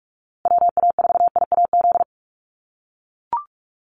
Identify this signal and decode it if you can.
Morse code, heard as a keyed tone, about 44 wpm, 700 Hz. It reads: WA4IAZ
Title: USA Call Signs Repeated 3x 45wpm